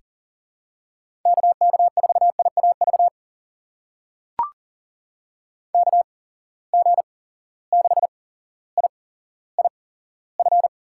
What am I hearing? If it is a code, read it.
Morse code, heard as a keyed tone, about 43 wpm, spaced as for 17 wpm, 700 Hz. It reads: KK4IAV KG6IIF
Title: USA Call Signs - 6 Characters - 8x Character Spacing - 40wpm